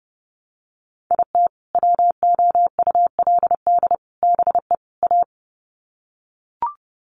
Morse code, heard as a keyed tone, about 30 wpm, 700 Hz. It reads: IT WOULD BE A
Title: Sets of 4 Words 30wpm